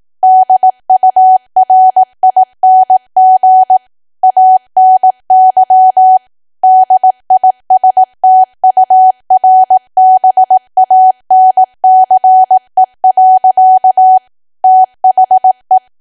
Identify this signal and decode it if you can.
Morse code, heard as a keyed tone, about 18 wpm, 750 Hz. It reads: DURING ANY DISTURBANCE. THE
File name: cwa / W1AW-Level2 / 18WPM